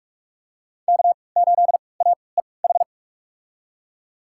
Morse code, heard as a keyed tone, about 46 wpm, spaced as for 26 wpm, 700 Hz. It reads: K8AEH
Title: USA Call Signs - 5 Characters - 3x Character Spacing - 45wpm